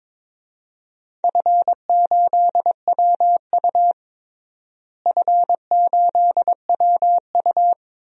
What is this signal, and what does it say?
Morse code, heard as a keyed tone, about 22 wpm, 700 Hz. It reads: F8WU F8WU
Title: DXCC-100 Call Signs - 4 Characters - Repeated 3x - 22wpm